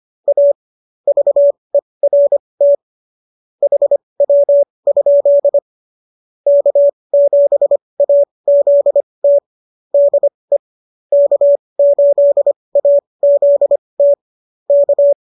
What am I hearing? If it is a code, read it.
Morse code, heard as a keyed tone, about 25 wpm, spaced as for 20 wpm, 570 Hz. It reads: A VERT HW? K7AZT DE K8AZT K